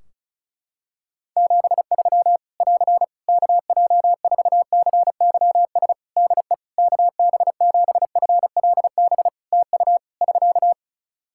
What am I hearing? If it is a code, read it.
Morse code, heard as a keyed tone, about 36 wpm, 700 Hz. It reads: 73 <AR> KJ4CYS DE KB7FLB TU <SK>
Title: Practice QSO - Exchange 4 - Ending - 35wpm